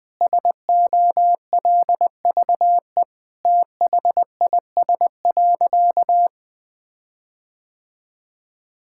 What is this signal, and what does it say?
Morse code, heard as a keyed tone, about 20 wpm, 700 Hz. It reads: SOLVE THIS.
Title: Sentences with 5-Word Phrases - Spoken Beginning - 20wpm